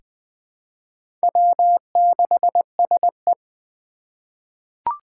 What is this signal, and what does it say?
Morse code, heard as a keyed tone, about 20 wpm, 700 Hz. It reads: W6SE
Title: USA Call Signs - 2x Character Spacing - 20wpm